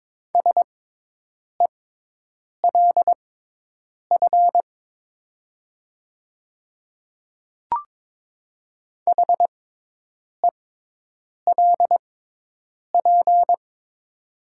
Morse code, heard as a keyed tone, about 22 wpm, spaced as for 8 wpm, 700 Hz. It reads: SELF HELP
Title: 6x Character Spacing - Top 200 Words - 22wpm